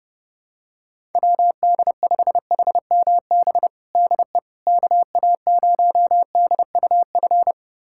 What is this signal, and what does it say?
Morse code, heard as a keyed tone, about 30 wpm, 700 Hz. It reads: WD5HMB DE KA0DUF
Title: USA Callsign DE Callsign 30wpm